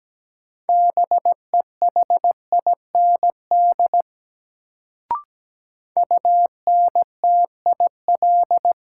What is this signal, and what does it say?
Morse code, heard as a keyed tone, about 17 wpm, 700 Hz. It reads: BEHIND UNTIL
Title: Sentences from Top 500 Words - New Word Drill - 17wpm